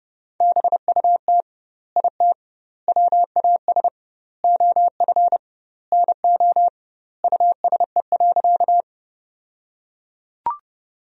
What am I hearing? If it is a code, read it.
Morse code, heard as a keyed tone, about 30 wpm, 700 Hz. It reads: BUT IT WAS OF NO USE.
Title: 2x Word-Spacing - Sentences from Top 100 Words - 30wpm